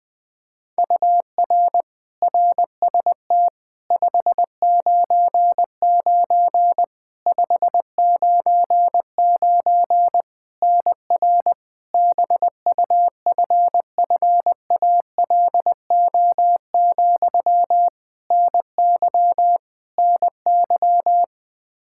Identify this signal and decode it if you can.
Morse code, heard as a keyed tone, about 20 wpm, 700 Hz. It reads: UR RST 599 599 NR BUFFALO, NY NY
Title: K1USN SST QSO Practice 20wpm